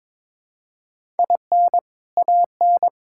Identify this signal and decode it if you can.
Morse code, heard as a keyed tone, about 22 wpm, 700 Hz. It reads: IN AN
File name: Build A Phrase - 2-Word Phrases - 22wpm